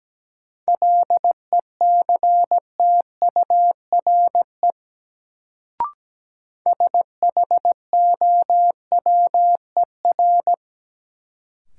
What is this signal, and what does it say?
Morse code, heard as a keyed tone, about 17 wpm, 700 Hz. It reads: LECTURE SHOWER
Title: Sentences from Top 1600 Words - New Word Drill Alt - 17wpm